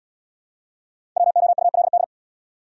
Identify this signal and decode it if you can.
Morse code, heard as a keyed tone, about 70 wpm, 690 Hz. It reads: WOULD